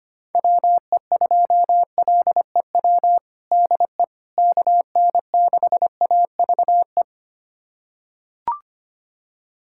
Morse code, heard as a keyed tone, about 25 wpm, 700 Hz. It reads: WE2LEW DE KN6AVE
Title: USA Callsign DE Callsign - 6 Characters - 25wpm